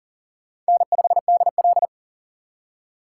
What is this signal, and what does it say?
Morse code, heard as a keyed tone, about 38 wpm, 690 Hz. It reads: N5DL
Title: USA Call Signs - 4x Character Spacing - 40wpm